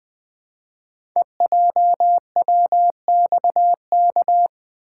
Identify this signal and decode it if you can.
Morse code, heard as a keyed tone, about 20 wpm, 700 Hz. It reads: EJWXK
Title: Code Groups - 5 Letters - 20wpm